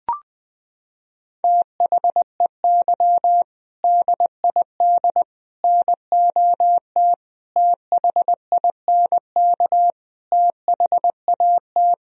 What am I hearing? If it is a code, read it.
Morse code, heard as a keyed tone, about 20 wpm, 700 Hz. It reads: THEY DID NOT THINK THAT
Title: Sentences from Top 1000 Words - Review Part 3 - 20wpm